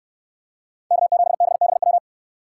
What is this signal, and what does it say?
Morse code, heard as a keyed tone, about 68 wpm, 700 Hz. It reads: K7BBW